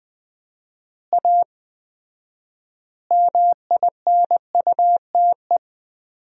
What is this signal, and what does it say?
Morse code, heard as a keyed tone, about 20 wpm, 700 Hz. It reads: A MINUTE
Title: 4x Word-Spacing - Sentences from Top 500 Words - 20wpm